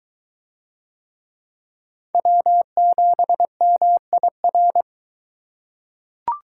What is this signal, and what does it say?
Morse code, heard as a keyed tone, about 21 wpm, 700 Hz. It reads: W7MIR